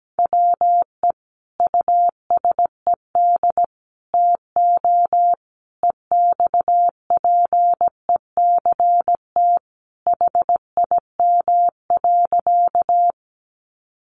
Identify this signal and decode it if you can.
Morse code, heard as a keyed tone, about 17 wpm, 700 Hz. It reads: WE USED TO EXPECT HIM.
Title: Sentences from Top 1100 Words 17wpm